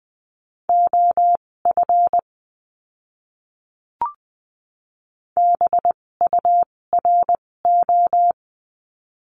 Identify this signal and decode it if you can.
Morse code, heard as a keyed tone, about 20 wpm, spaced as for 16 wpm, 700 Hz. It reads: OF BURO
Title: 256 Lesson 42a - Words 12-20wpm